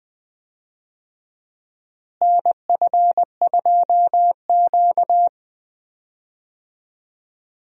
Morse code, heard as a keyed tone, about 20 wpm, 700 Hz. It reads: NF2Q
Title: USA Call Signs - 4 Characters - Repeated 3x - 20wpm